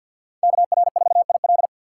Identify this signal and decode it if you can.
Morse code, heard as a keyed tone, about 52 wpm, 700 Hz. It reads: KR4IL
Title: USA Call Signs - 5 Characters - 4x Character Spacing - 50wpm